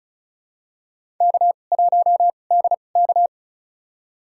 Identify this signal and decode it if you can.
Morse code, heard as a keyed tone, about 34 wpm, spaced as for 25 wpm, 700 Hz. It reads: K1DK
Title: USA Call Signs - 4 Characters - 2x Character Spacing - 35wpm